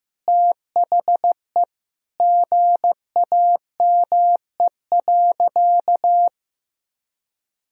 Morse code, heard as a keyed tone, about 15 wpm, 700 Hz. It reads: THE GAME.
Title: Sentences from Top 1000 Words - Review Part 4 - 15wpm